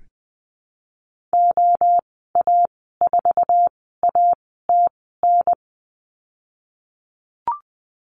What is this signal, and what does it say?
Morse code, heard as a keyed tone, about 19 wpm, spaced as for 14 wpm, 700 Hz. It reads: OA4ATN
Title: 048 Lesson 9g - Number 4 - Call Signs 20wpm